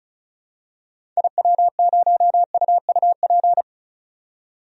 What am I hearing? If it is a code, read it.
Morse code, heard as a keyed tone, about 34 wpm, 700 Hz. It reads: IW0UUP
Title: DXCC-100 Call Signs - 6 Characters - 35wpm